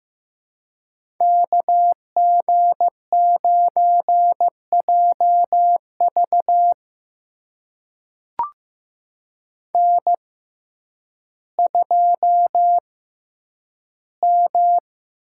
Morse code, heard as a keyed tone, about 15 wpm, spaced as for 9 wpm, 700 Hz. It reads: KG9JV N2M
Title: USA Call Signs - 6x Character Spacing - 15wpm